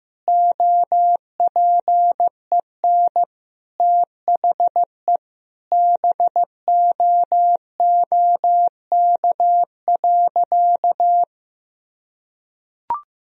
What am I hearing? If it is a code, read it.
Morse code, heard as a keyed tone, about 16 wpm, 700 Hz. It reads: OPEN THE BOOK.
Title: Sentences from Top 1000 Words - Review Part 1 - 15wpm